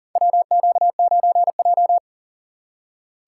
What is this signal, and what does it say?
Morse code, heard as a keyed tone, about 41 wpm, 700 Hz. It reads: WQ9J
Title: USA Call Signs - 6x Character Spacing - 40wpm